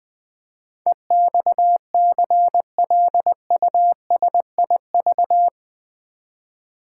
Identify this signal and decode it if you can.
Morse code, heard as a keyed tone, about 20 wpm, 700 Hz. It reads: EXCLUSIV
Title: Build A Word - 4 Letters At A Time - 11-Letter Words - 20wpm